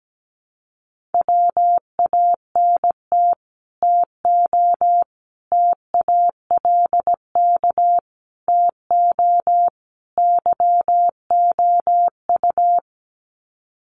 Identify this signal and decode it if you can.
Morse code, heard as a keyed tone, about 17 wpm, 700 Hz. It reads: WANT TO TALK TO YOU
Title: Sets of 5 Words Encore 17wpm